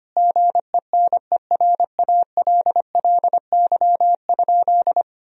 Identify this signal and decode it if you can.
Morse code, heard as a keyed tone, about 25 wpm, 700 Hz. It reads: GENERALLY?
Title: Sentences from Top 1000 Words - Review Part 4 - 25wpm